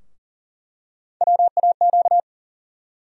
Morse code, heard as a keyed tone, about 38 wpm, 700 Hz. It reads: WAQ
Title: USA Call Sign Postfixes - 3 Letters 40wpm